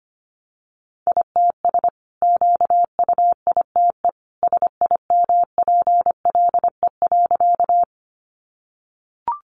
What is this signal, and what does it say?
Morse code, heard as a keyed tone, about 25 wpm, 700 Hz. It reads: ITS QUITE SIMPLE.
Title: Sentences from Top 700 Words 25wpm